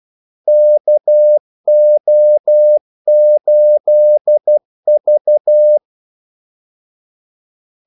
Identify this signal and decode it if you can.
Morse code, heard as a keyed tone, about 12 wpm, 600 Hz. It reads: KO8V
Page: 2 ekstra ordmellemrum mellem kaldesignalerne